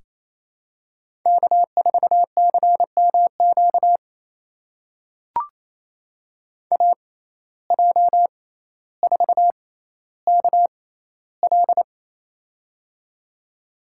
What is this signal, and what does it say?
Morse code, heard as a keyed tone, about 28 wpm, spaced as for 14 wpm, 700 Hz. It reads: K4CMQ AJ4KL
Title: USA Call Signs - 5 Characters - 6x Character Spacing - 28wpm